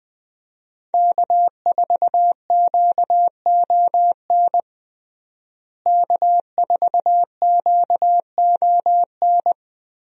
Morse code, heard as a keyed tone, about 21 wpm, 700 Hz. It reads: K4QON K4QON
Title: USA Call Signs - 5 Characters - Repeated 2x - 20wpm